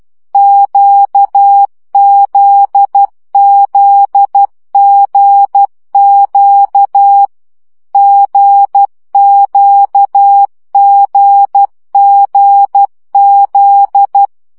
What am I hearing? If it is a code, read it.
Morse code, heard as a keyed tone, about 12 wpm, 800 Hz. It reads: QZZGQ GQGGZ
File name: CW gyakorlas g z q 10 baud